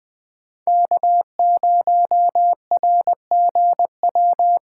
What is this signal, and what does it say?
Morse code, heard as a keyed tone, about 20 wpm, 700 Hz. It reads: K0RGW
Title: USA Call Signs - 5 Characters - 3x Character Spacing - 20wpm